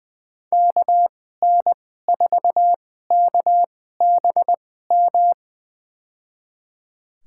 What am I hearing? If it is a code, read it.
Morse code, heard as a keyed tone, about 20 wpm, spaced as for 14 wpm, 700 Hz. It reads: KN4KBM
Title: USA Call Signs - 2x Character Spacing - 20wpm